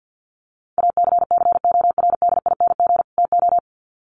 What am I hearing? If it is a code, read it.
Morse code, heard as a keyed tone, about 50 wpm, 700 Hz. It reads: ACCORDING TO
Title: Sets of 2 Words 50wpm